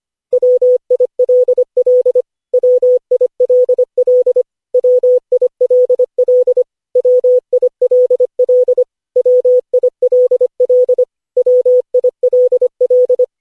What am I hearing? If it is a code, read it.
Morse code, heard as a keyed tone, about 25 wpm, 500 Hz. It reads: WILL WILL WILL WILL WILL WILL
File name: cw / Session5 cw training